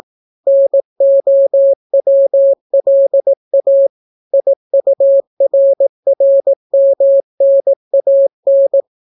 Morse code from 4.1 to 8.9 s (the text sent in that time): IURRMNAN